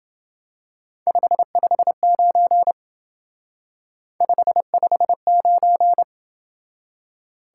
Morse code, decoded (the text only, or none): 559 559